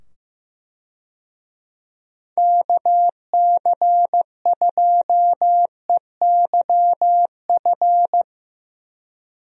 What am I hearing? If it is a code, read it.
Morse code, heard as a keyed tone, about 15 wpm, 700 Hz. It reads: KC2EYF
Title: USA Call Signs - 6 Characters - 15wpm